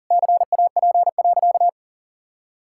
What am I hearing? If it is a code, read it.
Morse code, heard as a keyed tone, about 37 wpm, 700 Hz. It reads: CAP.
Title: Sentences from Top 1600 Words 40wpm